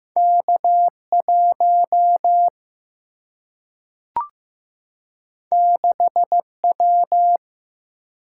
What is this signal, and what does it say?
Morse code, heard as a keyed tone, about 15 wpm, 700 Hz. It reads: K 1 6 W
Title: Single-Letter Single-Number 15wpm